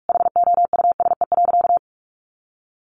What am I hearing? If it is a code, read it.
Morse code, heard as a keyed tone, about 44 wpm, 700 Hz. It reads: HOUSE.
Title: Sentences from Top 400 Words 45wpm